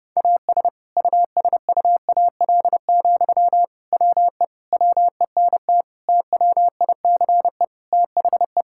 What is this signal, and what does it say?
Morse code, heard as a keyed tone, about 30 wpm, 700 Hz. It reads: AS USUAL, WE WENT TWICE THE
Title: Sentences from Top 1600 Words 30wpm